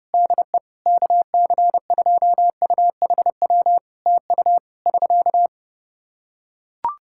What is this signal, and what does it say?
Morse code, heard as a keyed tone, about 30 wpm, 700 Hz. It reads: DE KC2UHW TU <SK>
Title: Practice QSO - Exchange 4 - Ending - 30wpm